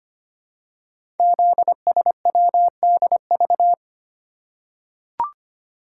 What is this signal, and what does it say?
Morse code, heard as a keyed tone, about 25 wpm, 700 Hz. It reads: ZSWDV
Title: Code Groups - 5 Letters - 25wpm